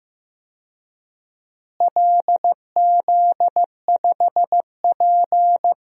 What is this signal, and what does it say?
Morse code, heard as a keyed tone, about 15 wpm, 700 Hz. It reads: LZ5P
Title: DXCC-100 Call Signs - 4 Characters - Repeated 2x - 15wpm